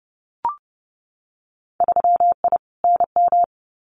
Morse code, heard as a keyed tone, about 30 wpm, 700 Hz. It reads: 3I NM
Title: Winter Field Day Exchanges Encore 30wpm